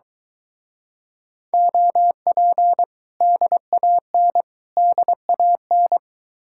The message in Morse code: OP DAN DAN

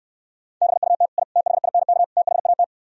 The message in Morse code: BUT I THINK THAT

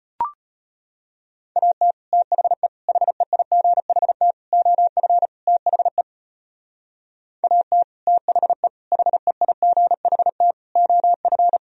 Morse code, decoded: AT THE HEIGHT OF THE AT THE HEIGHT OF

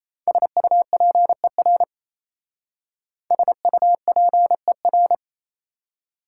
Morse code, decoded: SUPER SUPER

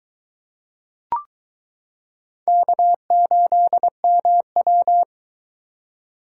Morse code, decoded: K8MW

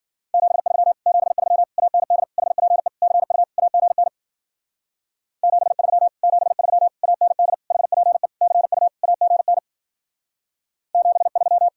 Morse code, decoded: 73 73 AND HPE CU AGN 73 73 AND HPE CU AGN 73

